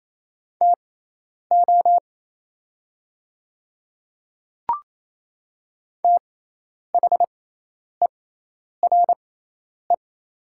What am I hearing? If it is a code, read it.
Morse code, heard as a keyed tone, about 27 wpm, spaced as for 10 wpm, 690 Hz. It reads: TO THERE
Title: 6x Character Spacing - Top 100 Words - 28wpm